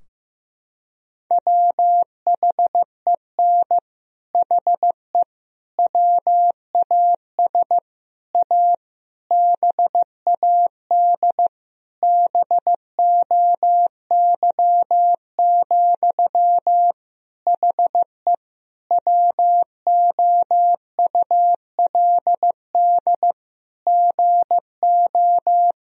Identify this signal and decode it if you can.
Morse code, heard as a keyed tone, about 16 wpm, 700 Hz. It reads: WHEN HE WAS A BAD BOY, HE WOULD GO
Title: Sentences from Top 1000 Words - Review Part 3 - 15wpm